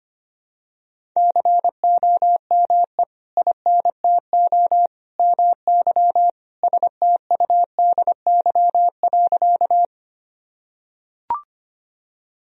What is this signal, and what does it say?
Morse code, heard as a keyed tone, about 25 wpm, 700 Hz. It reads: COME INTO MY STUDY.
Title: Sentences from Top 1000 Words - Review Part 1 - 25wpm